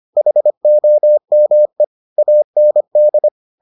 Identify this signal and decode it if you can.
Morse code, heard as a keyed tone, about 25 wpm, 600 Hz. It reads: HOME AND